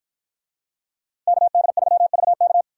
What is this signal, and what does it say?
Morse code, heard as a keyed tone, about 53 wpm, 700 Hz. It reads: KD3VK